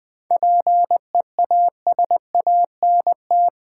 Morse code, heard as a keyed tone, about 21 wpm, 700 Hz. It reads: PEASANT
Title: 7 Letter Words - First 3 Letters Given - 20wpm